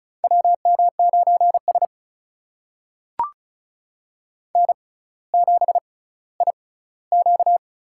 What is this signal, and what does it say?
Morse code, heard as a keyed tone, about 36 wpm, spaced as for 18 wpm, 700 Hz. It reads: WM9S N7IQ
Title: USA Call Signs - 4 Characters - 6x Character Spacing - 35wpm